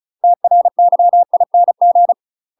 Morse code, heard as a keyed tone, about 34 wpm, 700 Hz. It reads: TRYING
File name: UK MASTER QSOs TRAINING/35WPM